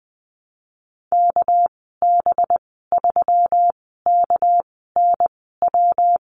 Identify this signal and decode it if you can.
Morse code, heard as a keyed tone, about 20 wpm, 700 Hz. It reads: K B 3 K N W
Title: 207 Lesson 33g - Letter j - Call Signs 20wpm